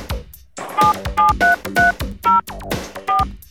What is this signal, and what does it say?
DTMF, heard as telephone touch tones, tones of about 160 ms. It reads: 77330 4